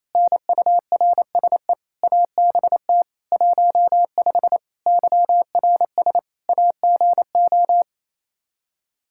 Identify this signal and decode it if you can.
Morse code, heard as a keyed tone, about 28 wpm, 700 Hz. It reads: NURSE ABT 15 YRS AGO
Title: K1USN SST QSO Practice 28wpm